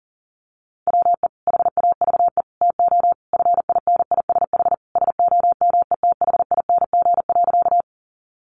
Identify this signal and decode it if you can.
Morse code, heard as a keyed tone, about 40 wpm, 700 Hz. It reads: WE HAVE TO FINISH SOMETHING.
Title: Sentences from Top 1200 Words 40wpm